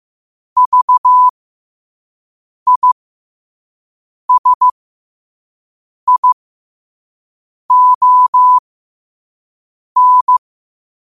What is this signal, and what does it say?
Morse code, heard as a keyed tone, about 15 wpm, spaced as for 5 wpm, 1000 Hz. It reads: VISION
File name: qotdfiles / 5wpm